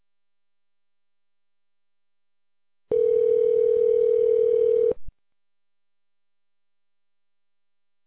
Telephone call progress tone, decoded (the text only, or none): ringback tone